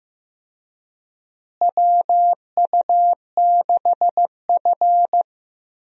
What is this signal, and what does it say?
Morse code, heard as a keyed tone, about 15 wpm, 700 Hz. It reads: WU6F
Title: USA Call Signs - 4 Characters - Repeated 2x - 15wpm